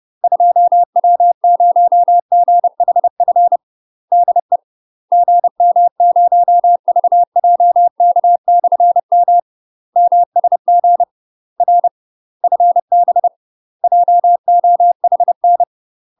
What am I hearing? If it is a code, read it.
Morse code, heard as a keyed tone, about 30 wpm, 700 Hz. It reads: 2W0GHF DE GM0VJK/M MSG R FB JOHN
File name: UK MASTER QSOs TRAINING / 30WPM / UK MASTER QSOs 0013